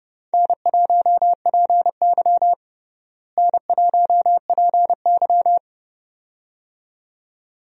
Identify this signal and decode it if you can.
Morse code, heard as a keyed tone, about 30 wpm, 700 Hz. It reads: N1PY N1PY
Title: USA Call Signs - 4 Characters - Repeated 2x - 30wpm